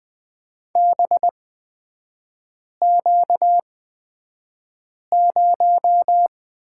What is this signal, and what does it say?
Morse code, heard as a keyed tone, about 20 wpm, spaced as for 5 wpm, 690 Hz. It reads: BQ0
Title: 236 Lesson 38e - Letter q - Three Characters 20wpm